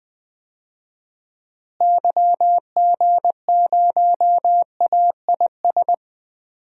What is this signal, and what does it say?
Morse code, heard as a keyed tone, about 21 wpm, 700 Hz. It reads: YG0AIS
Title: DXCC-100 Call Signs - 6 Characters - Repeated 3x - 20wpm